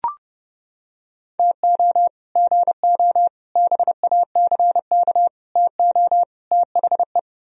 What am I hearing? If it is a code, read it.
Morse code, heard as a keyed tone, about 29 wpm, 700 Hz. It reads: TO GO BACK TO THE